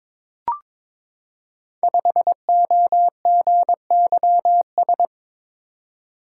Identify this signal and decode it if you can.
Morse code, heard as a keyed tone, about 22 wpm, 690 Hz. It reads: 5OGYS
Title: Code Groups - 5 Letter:Numbers - 22wpm